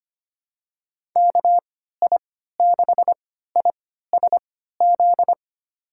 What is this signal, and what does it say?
Morse code, heard as a keyed tone, about 25 wpm, spaced as for 14 wpm, 700 Hz. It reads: KI6ISZ
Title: USA Call Signs - 6 Characters - 3x Character Spacing - 25wpm